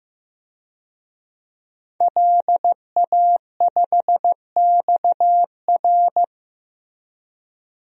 Morse code, heard as a keyed tone, about 16 wpm, 700 Hz. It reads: LA5XR
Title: DXCC-100 Call Signs - 5 Characters - Repeated 3x - 15wpm